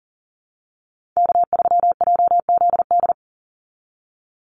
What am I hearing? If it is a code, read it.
Morse code, heard as a keyed tone, about 38 wpm, 700 Hz. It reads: K3JZD
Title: NA SOTA Chaser Callsigns 40wpm